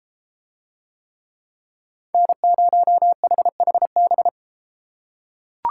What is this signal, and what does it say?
Morse code, heard as a keyed tone, about 31 wpm, 700 Hz. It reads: N0HHB